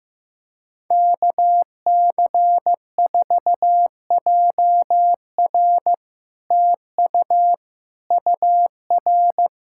Morse code, heard as a keyed tone, about 15 wpm, 700 Hz. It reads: KC4JR TU UR
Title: Speed-Racing Common QSO Phrases Encore - 15wpm